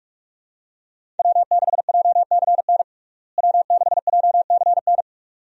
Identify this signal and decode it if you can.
Morse code, heard as a keyed tone, about 48 wpm, 700 Hz. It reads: W6JCN W6JCN
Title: USA Call Signs - 5 Characters - Repeated 2x - 45wpm